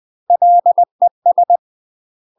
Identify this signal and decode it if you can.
Morse code, heard as a keyed tone, about 20 wpm, 700 Hz. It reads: LES
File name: UK MASTER QSOs TRAINING / 20WPM / UK MASTER QSOs 0042